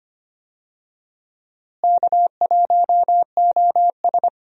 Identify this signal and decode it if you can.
Morse code, heard as a keyed tone, about 25 wpm, 700 Hz. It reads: K1OS
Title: USA Call Signs - 4 Characters - Repeated 3x - 25wpm